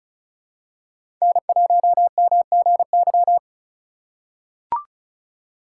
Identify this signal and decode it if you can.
Morse code, heard as a keyed tone, about 34 wpm, 700 Hz. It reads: N1MGY
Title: USA Call Signs - 4x Character Spacing - 35wpm